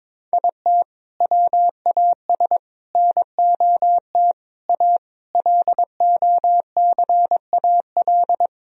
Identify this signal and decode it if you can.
Morse code, heard as a keyed tone, about 22 wpm, 700 Hz. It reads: IT WAS NOT A LOCAL
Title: Sentences from Top 1000 Words - Review Part 3 - 22wpm